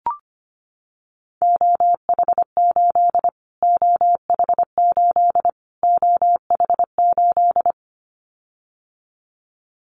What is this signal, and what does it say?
Morse code, heard as a keyed tone, about 25 wpm, 700 Hz. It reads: OH8 OH8 OH8
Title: Callsign Prefixes for Top 25 DXCC - Repeated 3x - 25wpm